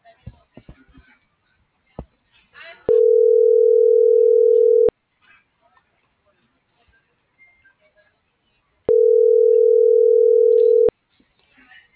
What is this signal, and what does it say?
Ringback tone (North America), 440 + 480 Hz.